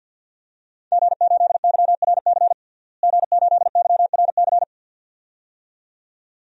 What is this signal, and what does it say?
Morse code, heard as a keyed tone, about 50 wpm, 700 Hz. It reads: G8YRC G8YRC
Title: DXCC-100 Call Signs - 5 Characters - Repeated 2x - 50wpm